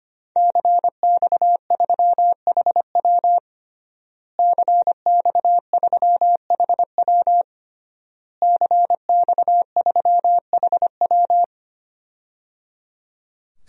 Morse code, heard as a keyed tone, about 25 wpm, 700 Hz. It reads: CX3HW CX3HW CX3HW
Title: DXCC-100 Call Signs - 5 Characters - Repeated 3x - 25wpm